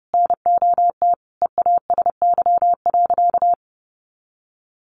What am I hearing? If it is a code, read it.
Morse code, heard as a keyed tone, about 30 wpm, 700 Hz. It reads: NOT EASY.
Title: Sentences from Top 1500 Words 30wpm